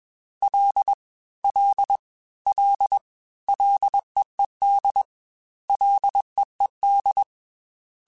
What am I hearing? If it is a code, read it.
Morse code, heard as a keyed tone, about 21 wpm, 800 Hz. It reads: L L L LEED LEED
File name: Ls / Single L m3s / Single L Prefixe 3x2 Twice @20